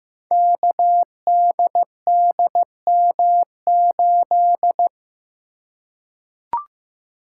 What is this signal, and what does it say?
Morse code, heard as a keyed tone, about 15 wpm, 700 Hz. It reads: KDDM8
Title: Code Groups - 5 Letter:Numbers - 15wpm